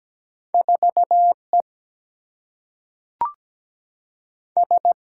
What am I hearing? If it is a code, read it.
Morse code, heard as a keyed tone, about 17 wpm, 700 Hz. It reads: 4 E S
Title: Single-Letter Single-Number 17wpm